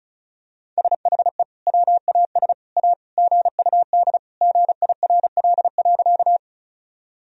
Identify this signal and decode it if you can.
Morse code, heard as a keyed tone, about 36 wpm, 700 Hz. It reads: SHE WAS A GUD GIRL.
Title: Sentences from Top 400 Words and Abbreviations 35wpm